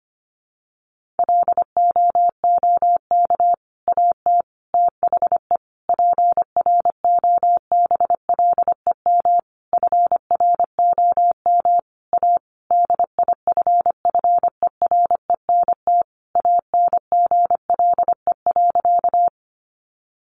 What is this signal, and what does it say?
Morse code, heard as a keyed tone, about 25 wpm, 700 Hz. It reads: LOOK AT THE PROBLEM FROM A DIFFERENT ANGLE.
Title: Sentences from Top 1500 Words 25wpm